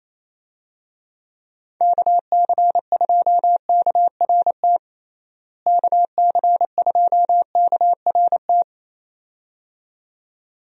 Morse code, heard as a keyed tone, about 27 wpm, 700 Hz. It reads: KC2KRT KC2KRT
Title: USA Call Signs - 6 Characters - Repeated 2x - 28wpm